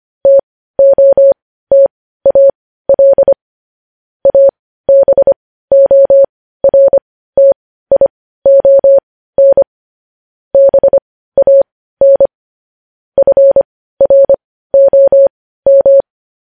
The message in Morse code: TOTAL ABORTION BAN FROM

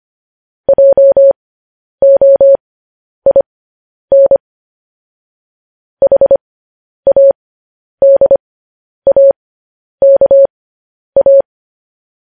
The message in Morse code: JOIN HADAKA

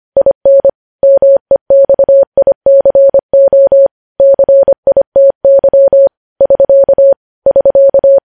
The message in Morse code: IN MEXICO CITY <SK> <SK>

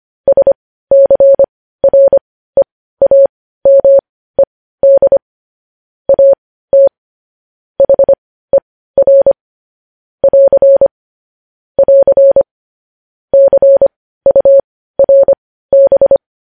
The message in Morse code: SCREAMED AT HER <AR> <AR> CURB

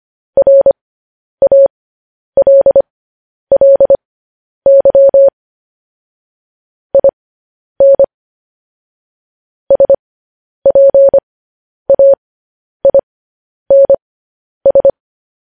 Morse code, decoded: RALLY IN SPAINS

